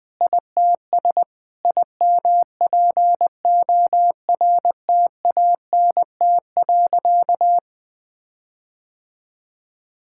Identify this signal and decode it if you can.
Morse code, heard as a keyed tone, about 20 wpm, 700 Hz. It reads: ITS IMPORTANT.
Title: Sentences with 6-Word Phrases - Spoken Beginning - 20wpm